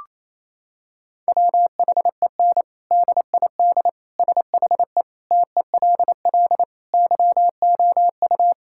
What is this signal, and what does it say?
Morse code, heard as a keyed tone, about 28 wpm, 700 Hz. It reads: WHEN DID SHE TELL YOU
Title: Sentences from Top 1000 Words - Review Part 6 - 28wpm